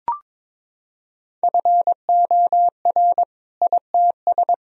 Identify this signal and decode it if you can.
Morse code, heard as a keyed tone, about 22 wpm, 700 Hz. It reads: FOR ITS
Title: Sentences from Top 1000 Words - Review Part 3 - 22wpm